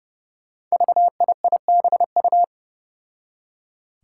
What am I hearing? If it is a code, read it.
Morse code, heard as a keyed tone, about 29 wpm, 700 Hz. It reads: VIIBU
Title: Five Challenging Letter-Number Combinations 30wpm